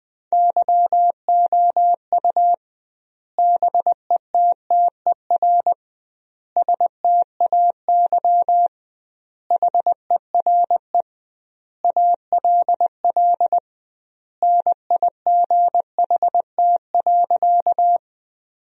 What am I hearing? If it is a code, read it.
Morse code, heard as a keyed tone, about 20 wpm, 700 Hz. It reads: YOU BETTER STAY HERE ALL NIGHT.
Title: 2x Word-Spacing - Sentences from Top 500 Words - 20wpm